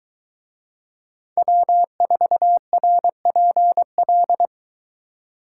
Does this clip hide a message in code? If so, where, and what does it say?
Yes: Morse code from 1.4 to 4.5 s, heard as a keyed tone, about 22 wpm, 700 Hz. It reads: W4RPL